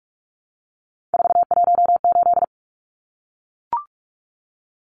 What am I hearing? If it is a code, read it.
Morse code, heard as a keyed tone, about 44 wpm, 700 Hz. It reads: 418
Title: Numbers - 3-Digit 45wpm